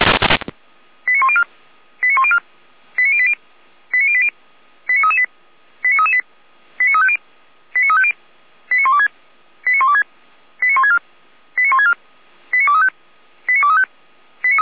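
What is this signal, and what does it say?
Five-tone selective call (ZVEI), heard as quick runs of five tones, tones of about 70 ms. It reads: 89104 89104 89080 89080 89338 89338 89350 89350 89126 89126 89164 89164 89235 89235
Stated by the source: ZVEI five-tone selective call received by SDR.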